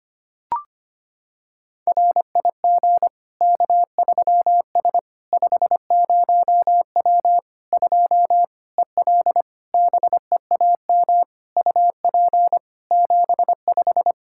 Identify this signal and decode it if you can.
Morse code, heard as a keyed tone, about 25 wpm, 700 Hz. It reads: RIG K3S 50W 2 EL BEAM UP 75
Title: Practice QSO - Exchange 2 - Info - 25wpm